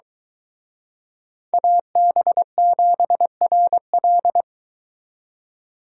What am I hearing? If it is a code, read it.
Morse code, heard as a keyed tone, about 23 wpm, 700 Hz. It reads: AB7RL